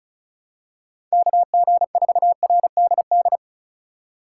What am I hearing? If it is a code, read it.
Morse code, heard as a keyed tone, about 36 wpm, 700 Hz. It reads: KG4RDD
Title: USA Call Signs - 6 Characters - 2x Character Spacing - 35wpm